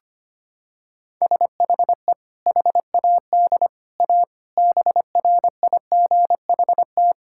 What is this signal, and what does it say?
Morse code, heard as a keyed tone, about 25 wpm, 700 Hz. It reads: SHE HAD A BRIGHT
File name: practice / 25wpm / Sentences with 2-Word Phrases - 25wpm